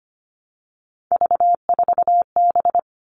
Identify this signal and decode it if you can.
Morse code, heard as a keyed tone, about 25 wpm, 700 Hz. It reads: V4B
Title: All Cut Numbers - 3-Digit 25wpm